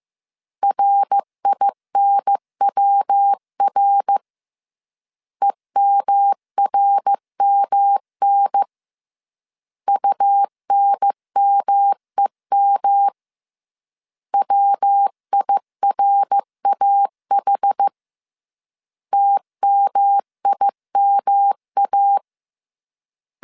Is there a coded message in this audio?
Yes: Morse code from 0.6 to 22.2 s, heard as a keyed tone, about 15 wpm, 800 Hz. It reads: RINWR EMRMN UNMEM WIRAH TMIMA